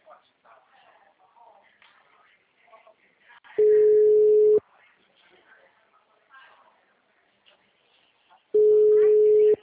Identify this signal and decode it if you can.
Ringback tone (Europe, 425 Hz), with speech.
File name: out- -2025